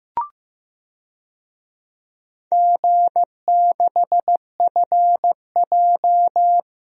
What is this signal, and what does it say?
Morse code, heard as a keyed tone, about 15 wpm, 700 Hz. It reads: G6FJ
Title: DXCC-100 Call Signs - 4 Characters - Repeated 3x - 15wpm